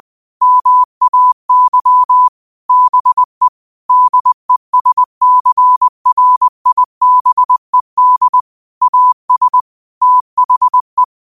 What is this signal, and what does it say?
Morse code, heard as a keyed tone, about 20 wpm, 1000 Hz. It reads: MAY BE DESCRIBED AS THE